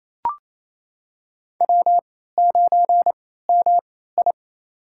Morse code, heard as a keyed tone, about 28 wpm, spaced as for 16 wpm, 700 Hz. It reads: W9MI
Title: USA Call Signs - 4 Characters - 3x Character Spacing - 28wpm